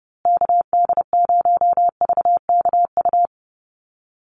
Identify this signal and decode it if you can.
Morse code, heard as a keyed tone, about 30 wpm, 700 Hz. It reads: KD0VKU
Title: USA Call Signs Repeated 2x 30wpm